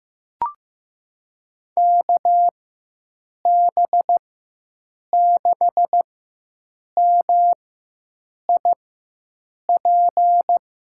Morse code, heard as a keyed tone, about 15 wpm, spaced as for 7 wpm, 700 Hz. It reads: KB6MIP
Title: USA Call Signs - 4x Character Spacing - 15wpm